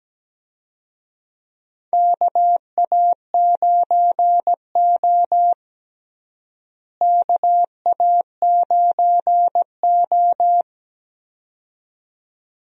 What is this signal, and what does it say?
Morse code, heard as a keyed tone, about 17 wpm, 700 Hz. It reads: KA9O KA9O
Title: USA Call Signs - 4 Characters - Repeated 2x - 17wpm